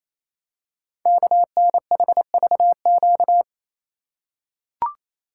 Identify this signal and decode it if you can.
Morse code, heard as a keyed tone, about 28 wpm, 700 Hz. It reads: KNHVQ
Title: Code Groups - 5 Letters - 28wpm